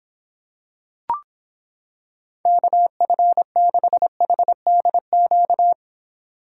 Morse code, heard as a keyed tone, about 26 wpm, 700 Hz. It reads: KF6HDQ